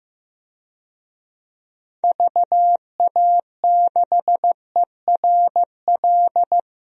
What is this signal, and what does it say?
Morse code, heard as a keyed tone, about 15 wpm, 700 Hz. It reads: VA6ERL
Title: DXCC-100 Call Signs - 6 Characters - Repeated 2x - 15wpm